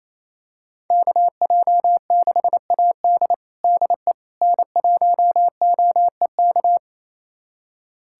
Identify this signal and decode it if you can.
Morse code, heard as a keyed tone, about 27 wpm, 700 Hz. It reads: KJ6AD DE N1OEK
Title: USA Callsign DE Callsign - 5 Characters - 28wpm